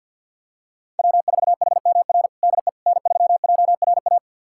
Morse code, heard as a keyed tone, about 52 wpm, 700 Hz. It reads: W4HMR DE N3JRA
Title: USA Callsign DE Callsign - 5 Characters - 50wpm